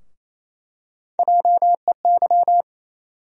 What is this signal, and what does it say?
Morse code, heard as a keyed tone, about 28 wpm, 700 Hz. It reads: JEY
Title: USA Call Sign Postfixes - 3 Letters 28wpm